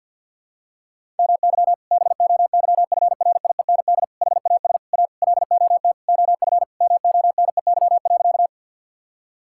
Morse code, heard as a keyed tone, about 52 wpm, 700 Hz. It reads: MY BOYFRIEND HAS A LOT OF MONEY.
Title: Sentences from Top 1000 Words - Review Part 5 - 50wpm